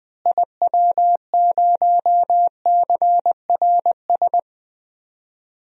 Morse code, heard as a keyed tone, about 20 wpm, 700 Hz. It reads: IW0CRS
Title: DXCC-100 Call Signs - 6 Characters - Repeated 3x - 20wpm